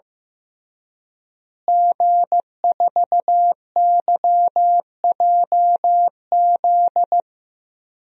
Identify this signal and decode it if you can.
Morse code, heard as a keyed tone, about 16 wpm, 700 Hz. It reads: G4YJZ
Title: DXCC-100 Call Signs - 5 Characters - Repeated 3x - 15wpm